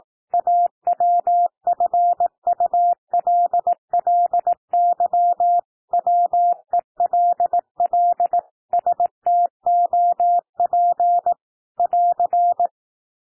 Morse code, heard as a keyed tone, about 18 wpm, 700 Hz. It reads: AWFULLY WELL STOP <AR>
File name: TEXT PASSAGE 18 wpm